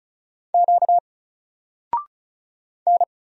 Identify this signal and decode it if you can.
Morse code, heard as a keyed tone, about 36 wpm, 700 Hz. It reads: Q N
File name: Single-Letter-Number-35wpm